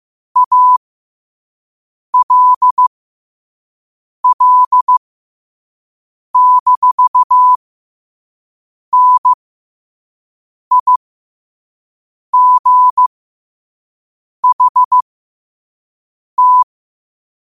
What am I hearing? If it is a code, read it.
Morse code, heard as a keyed tone, about 15 wpm, spaced as for 5 wpm, 1000 Hz. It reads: ALL-NIGHT
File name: qotdfiles / 5wpm